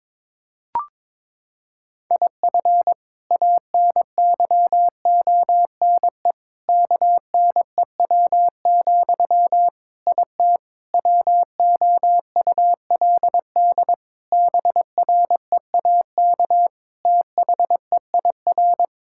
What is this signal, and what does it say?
Morse code, heard as a keyed tone, about 22 wpm, 700 Hz. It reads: IF ANYONE KNEW, IT WOULD BREAK THEIR
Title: Sentences from Top 1000 Words - Review Part 6 - 22wpm